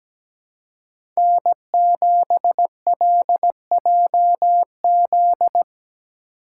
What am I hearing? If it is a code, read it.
Morse code, heard as a keyed tone, about 17 wpm, 700 Hz. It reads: N7LJZ
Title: USA Call Signs - 5 Characters - 2x Character Spacing - 17wpm